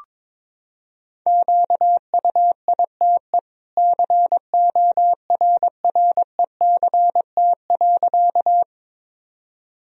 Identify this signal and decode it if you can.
Morse code, heard as a keyed tone, about 22 wpm, 700 Hz. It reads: QUITE CORRECT.
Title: Sentences from Top 1000 Words - Review Part 2 - 22wpm